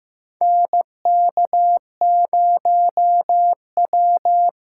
Morse code, heard as a keyed tone, about 15 wpm, 700 Hz. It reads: NK0W
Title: USA Callsign DE Callsign - 4 Characters - 15wpm